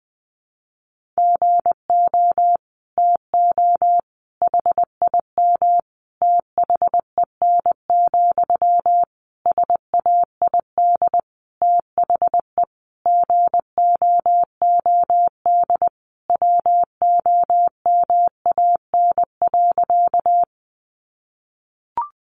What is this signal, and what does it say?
Morse code, heard as a keyed tone, about 20 wpm, 700 Hz. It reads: GO TO HIM THEN, SAID THE GOOD WOMAN.
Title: Sentences from Top 1100 Words 20wpm